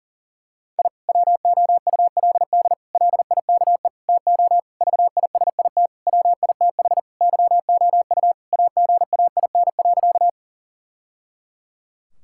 Morse code, heard as a keyed tone, about 40 wpm, 700 Hz. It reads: I WOULD LIKE TO VISIT WITH YOU AGAIN.
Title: Sentences from Top 1000 Words - Review Part 6 - 40wpm